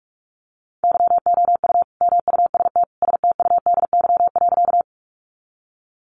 Morse code, heard as a keyed tone, about 44 wpm, 700 Hz. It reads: YOU MUST STUDY.
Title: Sentences from Top 300 Words 45wpm